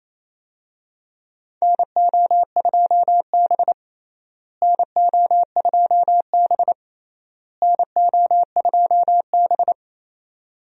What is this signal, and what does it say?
Morse code, heard as a keyed tone, about 28 wpm, 700 Hz. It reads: NO2B NO2B NO2B
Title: USA Call Signs - 4 Characters - Repeated 3x - 28wpm